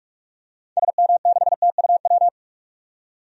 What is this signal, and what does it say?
Morse code, heard as a keyed tone, about 46 wpm, 700 Hz. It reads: SM6TUW
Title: DXCC-100 Call Signs - 6 Characters - 45wpm